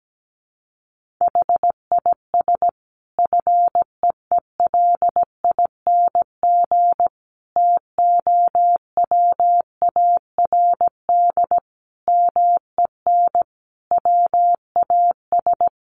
Morse code, heard as a keyed tone, about 17 wpm, 700 Hz. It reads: HIS FEELING TOWARD MEN WAS
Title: Sentences from Top 1300 Words 17wpm